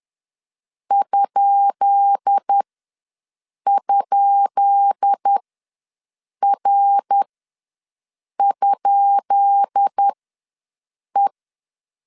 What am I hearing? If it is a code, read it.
Morse code, heard as a keyed tone, about 10 wpm, spaced as for 6 wpm, 800 Hz. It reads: ??R?E
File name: LEZIONI / 10 SIGNS / 10B INT ws 10wpm 4134